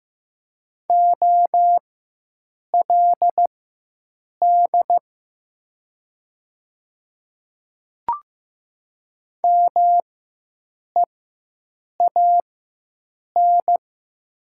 Morse code, heard as a keyed tone, about 15 wpm, spaced as for 7 wpm, 700 Hz. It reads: OLD MEAN
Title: 4x Character Spacing - Top 300 Words - 15wpm